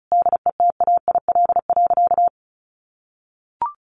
Morse code, heard as a keyed tone, about 34 wpm, 700 Hz. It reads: DETAIL.
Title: Sentences from Top 1500 Words 35wpm